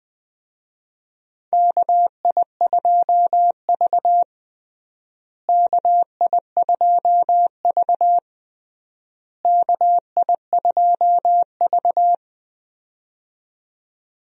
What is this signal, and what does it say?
Morse code, heard as a keyed tone, about 20 wpm, 700 Hz. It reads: KI2V KI2V KI2V
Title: USA Call Signs - 4 Characters - Repeated 3x - 20wpm